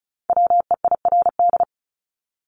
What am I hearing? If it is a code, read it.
Morse code, heard as a keyed tone, about 34 wpm, 700 Hz. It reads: WEIRD
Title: Top 1251-1500 Words 35wpm